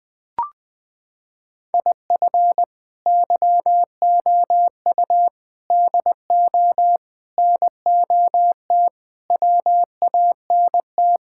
Morse code, heard as a keyed tone, about 20 wpm, 700 Hz. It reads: IF YOU DO NOT WANT
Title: Sentences with 5-Word Phrases - 20wpm